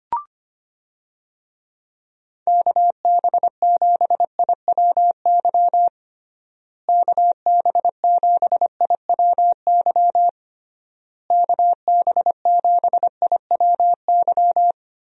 Morse code, heard as a keyed tone, about 25 wpm, 700 Hz. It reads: KB7IWY KB7IWY KB7IWY
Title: USA Call Signs - 6 Characters - Repeated 3x - 25wpm